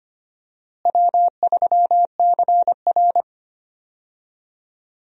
Morse code, heard as a keyed tone, about 25 wpm, 700 Hz. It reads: W3CR
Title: USA Call Signs - 4 Characters - Repeated 2x - 25wpm